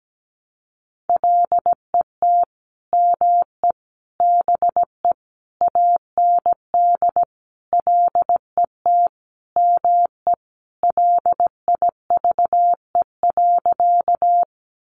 Morse code, heard as a keyed tone, about 17 wpm, 700 Hz. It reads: LET ME BE AND LET ME LIVE.
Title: Sentences from Top 300 Words 17wpm